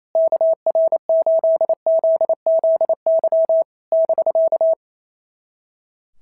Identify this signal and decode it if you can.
Morse code, heard as a keyed tone, about 28 wpm, 690 Hz. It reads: KR8ZZY <BK>
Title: SOTA QSO Practice 28wpm